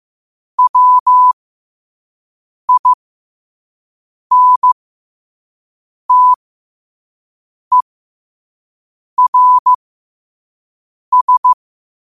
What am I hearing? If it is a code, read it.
Morse code, heard as a keyed tone, about 15 wpm, spaced as for 5 wpm, 1000 Hz. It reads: WINTERS
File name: qotdfiles / 5wpm